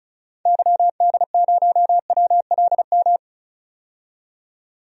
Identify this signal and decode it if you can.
Morse code, heard as a keyed tone, about 34 wpm, 700 Hz. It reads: YD0WLM
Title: DXCC-100 Call Signs - 6 Characters - 35wpm